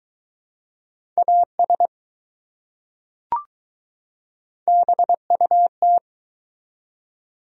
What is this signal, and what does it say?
Morse code, heard as a keyed tone, about 20 wpm, 700 Hz. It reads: AS BUT